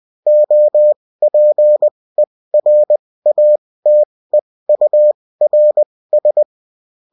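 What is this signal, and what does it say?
Morse code, heard as a keyed tone, about 20 wpm, spaced as for 16 wpm, 600 Hz. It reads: OPERATEURS
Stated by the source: Bulletin in text and CW form.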